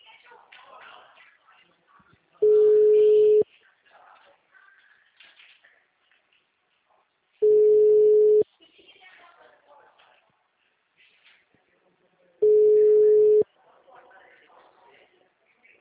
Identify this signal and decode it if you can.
Ringback tone (Europe, 425 Hz), with speech.
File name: out- -2006- - - .58714